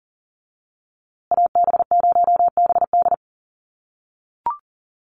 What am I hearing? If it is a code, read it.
Morse code, heard as a keyed tone, about 40 wpm, 700 Hz. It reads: AB0BD
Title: USA Call Signs Repeated 2x 40wpm